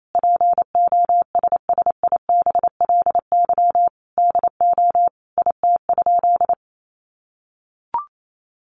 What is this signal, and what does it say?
Morse code, heard as a keyed tone, about 28 wpm, 700 Hz. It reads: POSSIBLY DO IT?
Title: Sentences from Top 1300 Words 28wpm